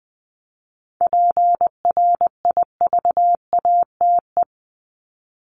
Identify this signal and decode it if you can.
Morse code, heard as a keyed tone, about 20 wpm, 700 Hz. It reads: PRIVATE
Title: Top 751-1000 Words 20wpm